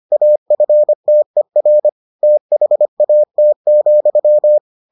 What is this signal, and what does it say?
Morse code, heard as a keyed tone, about 25 wpm, 600 Hz. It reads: AFTER THAT,